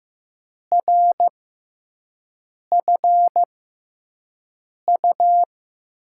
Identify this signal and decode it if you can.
Morse code, heard as a keyed tone, about 15 wpm, spaced as for 5 wpm, 700 Hz. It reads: RFU
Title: USA Call Signs - 5 Characters - 6x Character Spacing - 15wpm